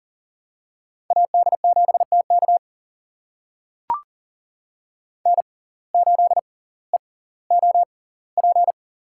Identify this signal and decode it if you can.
Morse code, heard as a keyed tone, about 40 wpm, spaced as for 20 wpm, 700 Hz. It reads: AD7TK N8EOP
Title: USA Call Signs - 5 Characters - 6x Character Spacing - 40wpm